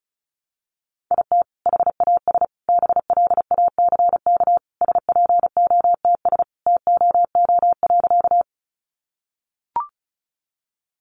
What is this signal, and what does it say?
Morse code, heard as a keyed tone, about 34 wpm, 700 Hz. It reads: IT HAS BLACK SPOTS TOO.
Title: Additional Easy Phrases 35wpm